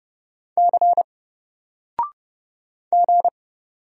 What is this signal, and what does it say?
Morse code, heard as a keyed tone, about 30 wpm, 700 Hz. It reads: C G
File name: Single-Letter-Number-30wpm